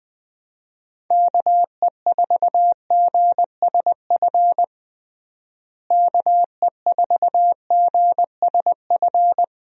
Morse code, heard as a keyed tone, about 21 wpm, 700 Hz. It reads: KE4GSF KE4GSF
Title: USA Call Signs - 6 Characters - Repeated 2x - 20wpm